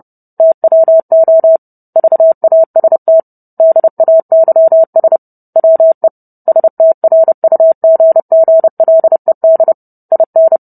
Morse code, heard as a keyed tone, about 30 wpm, 650 Hz. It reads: TWO VAST DAYS WE STRUGGLED IN